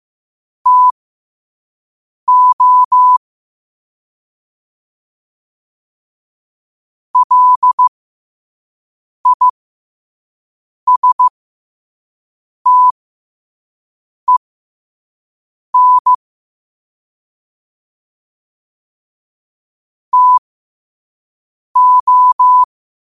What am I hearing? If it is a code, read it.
Morse code, heard as a keyed tone, about 15 wpm, spaced as for 5 wpm, 1000 Hz. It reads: TO LISTEN TO
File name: qotdfiles / 5wpm